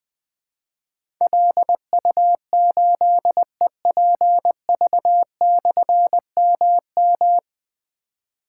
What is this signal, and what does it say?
Morse code, heard as a keyed tone, about 20 wpm, 700 Hz. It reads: LU8EPV/MM
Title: Difficult Call Signs 20wpm